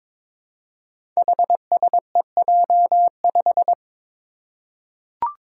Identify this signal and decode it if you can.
Morse code, heard as a keyed tone, about 22 wpm, 700 Hz. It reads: HSEJ5
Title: Five Challenging Letter-Number Combinations 22wpm